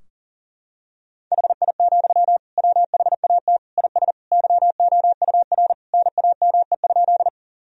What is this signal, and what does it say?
Morse code, heard as a keyed tone, about 40 wpm, 700 Hz. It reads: HI, WHAT IS YOUR NAME?
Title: Sentences from Top 1000 Words - Review Part 3 - 40wpm